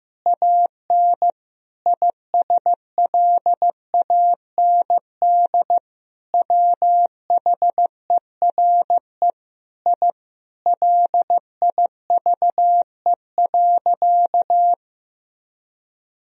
Morse code, heard as a keyed tone, about 16 wpm, 700 Hz. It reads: AN ISLAND WHERE I LIVE.
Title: Sentences from Top 1000 Words - Review Part 3 - 15wpm